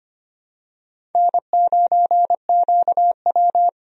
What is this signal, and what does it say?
Morse code, heard as a keyed tone, about 25 wpm, 700 Hz. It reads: N9QW
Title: USA Call Signs - 4 Characters - Repeated 2x - 25wpm